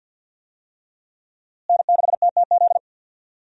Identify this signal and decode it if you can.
Morse code, heard as a keyed tone, about 52 wpm, 700 Hz. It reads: N6TTZ
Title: USA Call Signs - 5 Characters - Repeated 3x - 50wpm